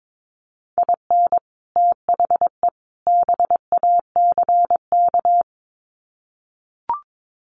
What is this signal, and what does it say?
Morse code, heard as a keyed tone, about 22 wpm, 700 Hz. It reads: IN THE BACK
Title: Sets of 3 Words 22wpm